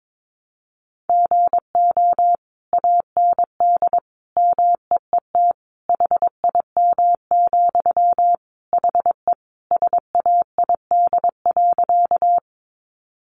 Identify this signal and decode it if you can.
Morse code, heard as a keyed tone, about 22 wpm, 700 Hz. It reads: GO AND MEET HIM, HE SAID.
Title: Sentences from Top 1100 Words 22wpm